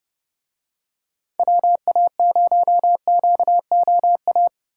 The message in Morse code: WA0QOA

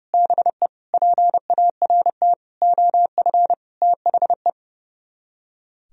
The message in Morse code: BE PART OF THE

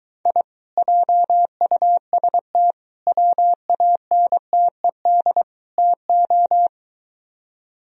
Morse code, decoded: I JUST WANTED TO

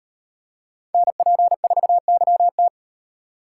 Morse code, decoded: NP4YT

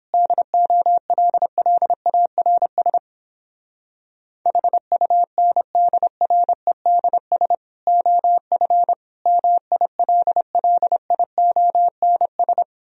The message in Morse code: DOLLARS HUNDREDS OF MILLIONS

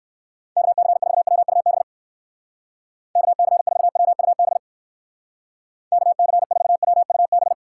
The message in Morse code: KC4WUB KC4WUB KC4WUB

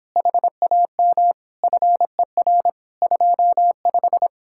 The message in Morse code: HAM FER 25

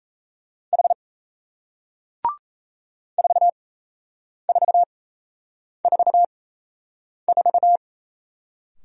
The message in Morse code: H 4444

